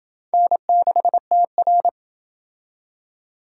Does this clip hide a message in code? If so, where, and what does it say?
Yes: Morse code from 0.3 to 1.9 s, heard as a keyed tone, about 27 wpm, 700 Hz. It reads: N6TR